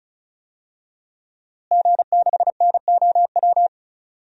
Morse code, heard as a keyed tone, about 34 wpm, 700 Hz. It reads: G6NOW
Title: DXCC-100 Call Signs - 5 Characters - Repeated 3x - 35wpm